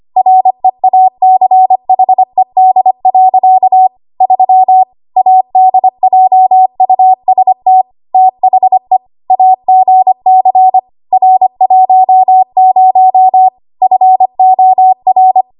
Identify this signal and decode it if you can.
Morse code, heard as a keyed tone, about 25 wpm, 750 Hz. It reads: REACHED. 3 ADJUST THE AGC R10 FOR
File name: Archive / 25 WPM / 25WPM